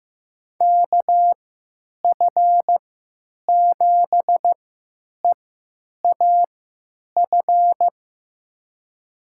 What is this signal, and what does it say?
Morse code, heard as a keyed tone, about 15 wpm, spaced as for 9 wpm, 700 Hz. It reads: KF7EAF
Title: USA Call Signs - 3x Character Spacing - 15wpm